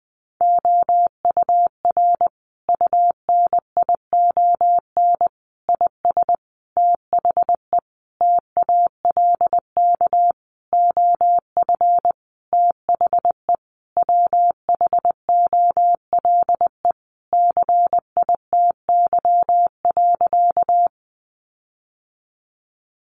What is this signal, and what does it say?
Morse code, heard as a keyed tone, about 19 wpm, 700 Hz. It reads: OUR UNION IS THE TALK OF THE WHOLE CITY.
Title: Sentences from Top 1400 Words 20wpm